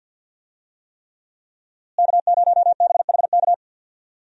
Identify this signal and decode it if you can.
Morse code, heard as a keyed tone, about 51 wpm, 700 Hz. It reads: K0BHK
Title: USA Call Signs - 5 Characters - Repeated 3x - 50wpm